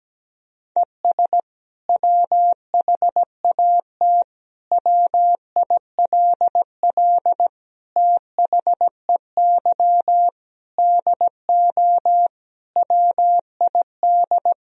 Morse code, heard as a keyed tone, about 17 wpm, 700 Hz. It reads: ES WHAT WILL THEY DO WID
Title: Sentences from Top 200 Words and Abbreviations 17wpm